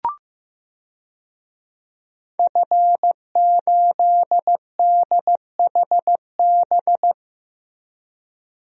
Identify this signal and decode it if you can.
Morse code, heard as a keyed tone, about 15 wpm, 700 Hz. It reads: F8DHB
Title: DXCC-100 Call Signs - 5 Characters - Repeated 3x - 15wpm